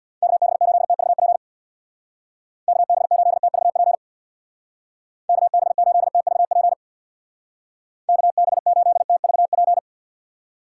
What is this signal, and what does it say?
Morse code, heard as a keyed tone, about 61 wpm, 700 Hz. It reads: KB8TVP KB8TVP KB8TVP KB8TVP
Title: Speed Racing - USA Call Signs 50wpm